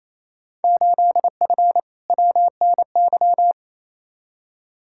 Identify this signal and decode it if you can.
Morse code, heard as a keyed tone, about 28 wpm, 700 Hz. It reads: 8F WNY
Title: Field Day Exchanges Encore - 28wpm